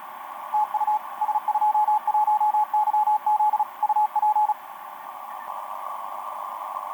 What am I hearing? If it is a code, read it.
Morse code, heard as a keyed tone, about 36 wpm, 850 Hz, with radio noise. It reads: TU S21YZ UP
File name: s21yz ba8ba etc 12m cw 22mar2011 0925